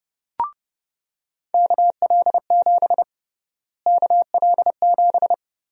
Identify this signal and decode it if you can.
Morse code, heard as a keyed tone, about 30 wpm, 700 Hz. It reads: KL7 KL7
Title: USA Call Sign Prefixes - Repeated 2x - 30wpm